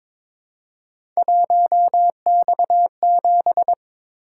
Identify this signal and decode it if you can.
Morse code, heard as a keyed tone, about 23 wpm, 700 Hz. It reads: 1X7
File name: practice / 22wpm / Three Letter-Number Combinations - 22wpm